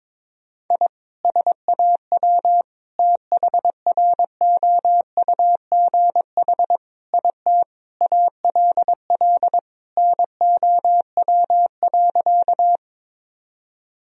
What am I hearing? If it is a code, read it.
Morse code, heard as a keyed tone, about 22 wpm, 700 Hz. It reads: I SAW THROUGH IT ALL NOW.
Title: Sentences from Top 1000 Words - Review Part 1 - 22wpm